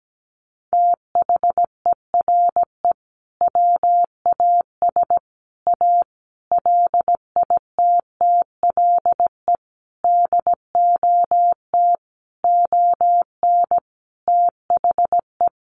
Morse code, heard as a keyed tone, about 17 wpm, 700 Hz. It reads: THERE WAS A LITTLE DOT ON THE
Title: Sentences from Top 1500 Words 17wpm